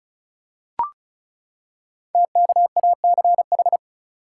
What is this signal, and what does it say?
Morse code, heard as a keyed tone, about 36 wpm, 700 Hz. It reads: TKACH
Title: Code Groups - 5 Letters - 35wpm